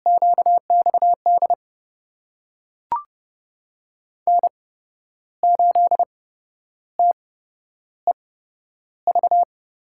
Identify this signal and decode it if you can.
Morse code, heard as a keyed tone, about 30 wpm, spaced as for 11 wpm, 700 Hz. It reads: QXD N8TEV
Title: USA Call Signs - 5 Characters - 8x Character Spacing - 30wpm